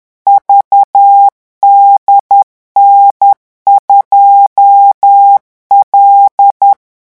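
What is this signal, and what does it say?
Morse code, heard as a keyed tone, about 10 wpm, 800 Hz. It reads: VDN2L
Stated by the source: Morse revision exercise for French radio amateurs at 10 words per minute.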